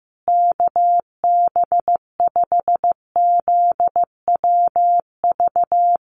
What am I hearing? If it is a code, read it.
Morse code, heard as a keyed tone, about 15 wpm, 700 Hz. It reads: KB5ZWV
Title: USA Call Signs Repeated 3x 15wpm